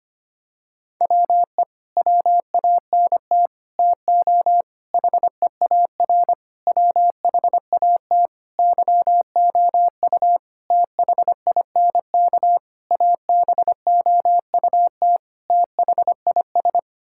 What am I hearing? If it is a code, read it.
Morse code, heard as a keyed tone, about 25 wpm, 700 Hz. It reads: WE WANT TO HEAR WHAT YOU THINK ABOUT THIS
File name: Sets of 9 Words - 25wpm